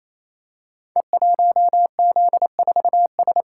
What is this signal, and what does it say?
Morse code, heard as a keyed tone, about 27 wpm, 700 Hz. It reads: E1Z4S
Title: Code Groups - 5 Letter:Numbers - 28wpm